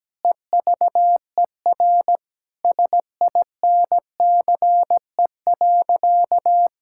Morse code, heard as a keyed tone, about 17 wpm, 700 Hz. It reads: EVER SINCE.
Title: Sentences from Top 1000 Words - Review Part 5 - 17wpm